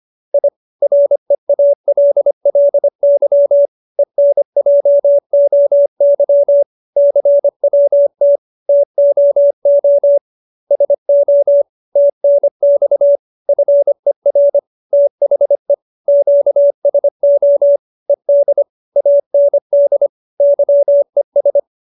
Morse code, heard as a keyed tone, about 25 wpm, 570 Hz. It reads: I REALLY ENJOY CWT TOO SO TNX FER THE QSO ED AND YES